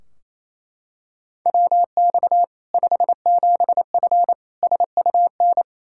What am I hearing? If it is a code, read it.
Morse code, heard as a keyed tone, about 28 wpm, 700 Hz. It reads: WX 57F SUN
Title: K1USN SST QSO Practice 28wpm